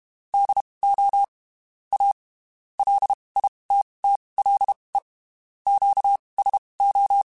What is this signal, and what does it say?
Morse code, heard as a keyed tone, about 32 wpm, spaced as for 23 wpm, 800 Hz. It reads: DO A LITTLE QSO
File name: (spaced) / QSOs Spaced-mp3s / WD0FBT @30 sp